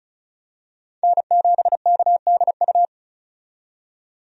Morse code, heard as a keyed tone, about 36 wpm, 700 Hz. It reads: N7KDU
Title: USA Call Signs - 5 Characters - 35wpm